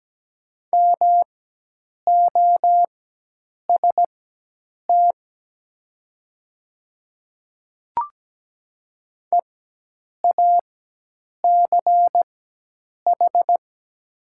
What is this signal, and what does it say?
Morse code, heard as a keyed tone, about 17 wpm, spaced as for 8 wpm, 700 Hz. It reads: MOST EACH
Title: 4x Character Spacing - Top 100 Words - 17wpm